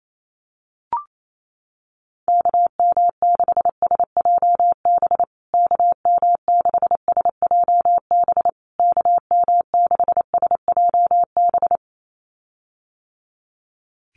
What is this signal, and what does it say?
Morse code, heard as a keyed tone, about 28 wpm, 700 Hz. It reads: KM6SJB KM6SJB KM6SJB
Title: USA Call Signs Repeated 3x 28wpm